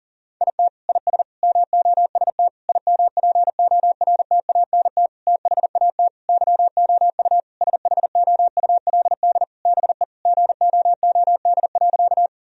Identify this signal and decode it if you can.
Morse code, heard as a keyed tone, about 37 wpm, 700 Hz. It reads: IT IS MOST IMPORTANT THAT YOU SHOULD BE GOOD.
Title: Sentences from Top 1000 Words - Review Part 1 - 40wpm